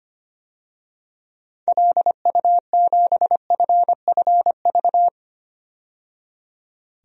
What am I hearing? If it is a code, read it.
Morse code, heard as a keyed tone, about 25 wpm, 700 Hz. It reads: LU7FFV
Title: DXCC-100 Call Signs - 6 Characters - 25wpm